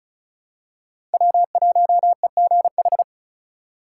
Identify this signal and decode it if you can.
Morse code, heard as a keyed tone, about 36 wpm, 700 Hz. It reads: W1EGH
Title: USA Call Signs - 6x Character Spacing - 35wpm